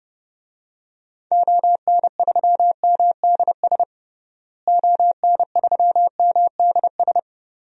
Morse code, heard as a keyed tone, about 30 wpm, 700 Hz. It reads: ON3MDS ON3MDS
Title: DXCC-100 Call Signs - 6 Characters - Repeated 2x - 30wpm